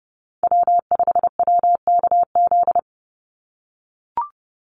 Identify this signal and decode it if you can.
Morse code, heard as a keyed tone, about 30 wpm, 700 Hz. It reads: W5WKZ
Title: USA Call Signs 30wpm